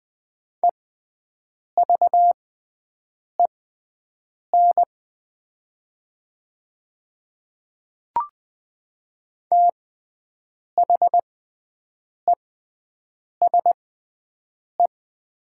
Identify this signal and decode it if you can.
Morse code, heard as a keyed tone, about 20 wpm, spaced as for 7 wpm, 700 Hz. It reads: EVEN THESE
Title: 6x Character Spacing - Top 200 Words - 20wpm